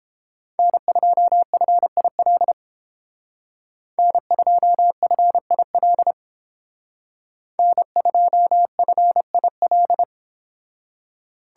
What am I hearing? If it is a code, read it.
Morse code, heard as a keyed tone, about 31 wpm, 700 Hz. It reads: N2FIL N2FIL N2FIL